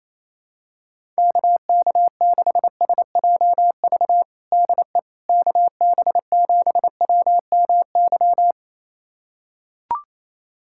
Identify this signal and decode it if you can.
Morse code, heard as a keyed tone, about 28 wpm, 700 Hz. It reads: KK6SJV DE KB7WMY
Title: USA Callsign DE Callsign - 6 Characters - 28wpm